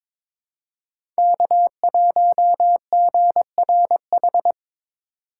Morse code, heard as a keyed tone, about 22 wpm, 700 Hz. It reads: K1GRH
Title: USA Call Signs - 5 Characters - 4x Character Spacing - 22wpm